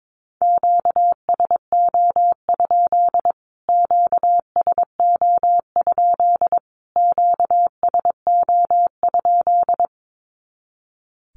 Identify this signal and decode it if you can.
Morse code, heard as a keyed tone, about 22 wpm, 700 Hz. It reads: QSO? QSO? QSO?
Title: Q-Codes Repeated 3x 22wpm